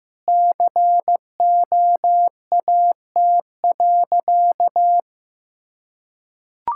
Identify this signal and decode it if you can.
Morse code, heard as a keyed tone, about 15 wpm, 700 Hz. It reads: COAT.
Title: Sentences from Top 1600 Words 15wpm